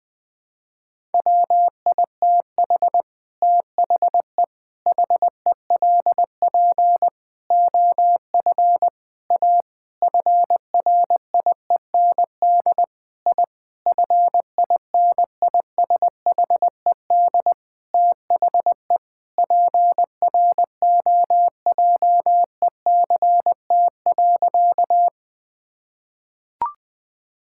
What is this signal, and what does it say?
Morse code, heard as a keyed tone, about 21 wpm, 700 Hz. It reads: WITH THE HELP OF A FRIEND I FINISHED THE PROJECT.
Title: Sentences with 5-Word Phrases - 20wpm